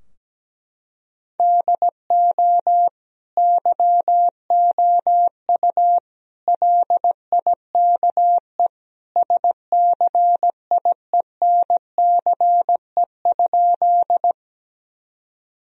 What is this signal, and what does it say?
Morse code, heard as a keyed tone, about 17 wpm, 700 Hz. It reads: DO YOU LIKE SCIENCE?
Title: Sentences from Top 1000 Words - Review Part 2 - 17wpm